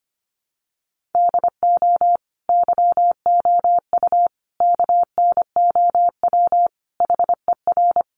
Morse code, heard as a keyed tone, about 25 wpm, 700 Hz. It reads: DO YOU KNOW HER
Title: Sentences from Top 1400 Words 25wpm